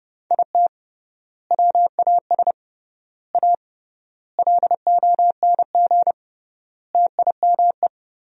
Morse code, heard as a keyed tone, about 32 wpm, 700 Hz. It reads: IT WAS A LONG TIME
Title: 3x Word-Spacing - Sentences from Top 500 Words - 30wpm